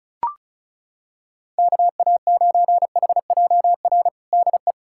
Morse code, heard as a keyed tone, about 34 wpm, 700 Hz. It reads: KA9HJR DE
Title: Practice QSO - Exchange 4 - Ending - 35wpm